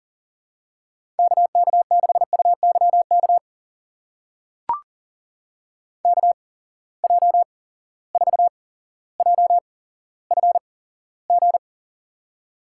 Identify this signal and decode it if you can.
Morse code, heard as a keyed tone, about 40 wpm, spaced as for 17 wpm, 700 Hz. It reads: KK6UYK KJ4JFG
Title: USA Call Signs - 6 Characters - 8x Character Spacing - 40wpm